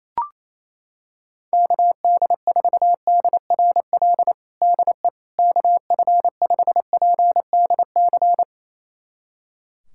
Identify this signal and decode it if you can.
Morse code, heard as a keyed tone, about 27 wpm, 700 Hz. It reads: KD4DRL DE KF5PDC
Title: USA Callsign DE Callsign - 6 Characters - 28wpm